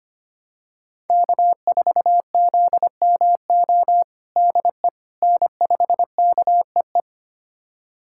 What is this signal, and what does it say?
Morse code, heard as a keyed tone, about 25 wpm, 700 Hz. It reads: K4ZMO DE N5KEE
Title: USA Callsign DE Callsign - 5 Characters - 25wpm